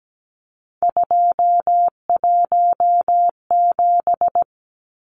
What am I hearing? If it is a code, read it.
Morse code, heard as a keyed tone, about 17 wpm, 700 Hz. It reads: 217
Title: Numbers - 3-Digit 17wpm